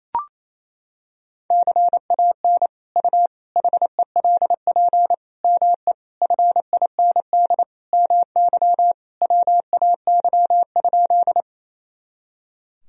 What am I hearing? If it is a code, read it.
Morse code, heard as a keyed tone, about 28 wpm, 700 Hz. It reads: CAN U HELP ME FIND MY WAY?
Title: Sentences from Top 200 Words and Abbreviations 28wpm